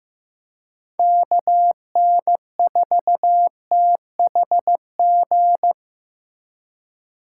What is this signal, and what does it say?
Morse code, heard as a keyed tone, about 15 wpm, 700 Hz. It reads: KN4THG
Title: USA Call Signs - 6 Characters - 2x Character Spacing - 15wpm